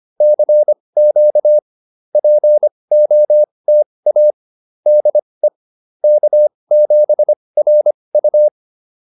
Morse code, heard as a keyed tone, about 25 wpm, spaced as for 20 wpm, 600 Hz. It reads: CQ POTA DE K7RU
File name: POTA206 20